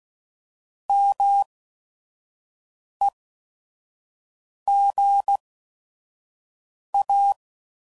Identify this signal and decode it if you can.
Morse code, heard as a keyed tone, about 16 wpm, spaced as for 5 wpm, 800 Hz. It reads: MEGA